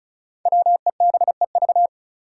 Morse code, heard as a keyed tone, about 36 wpm, 700 Hz. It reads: WEBEV
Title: Five Challenging Letter-Number Combinations 35wpm